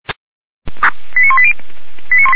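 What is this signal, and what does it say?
Five-tone selective call (ZVEI), heard as quick runs of five tones, tones of about 70 ms. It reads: 89180
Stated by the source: ZVEI five-tone selective call 89135 received with an SDR, Feuerwehr Orlamünde fire brigade.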